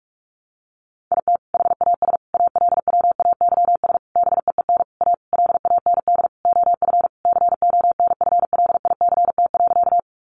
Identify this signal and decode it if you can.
Morse code, heard as a keyed tone, about 44 wpm, 700 Hz. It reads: IT HAS ALWAYS BEEN A LAND OF CONFLICT.
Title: Sentences from Top 1300 Words 45wpm